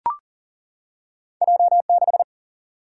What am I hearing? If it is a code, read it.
Morse code, heard as a keyed tone, about 40 wpm, 700 Hz. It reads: J6
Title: Single-Letter Single-Number 40wpm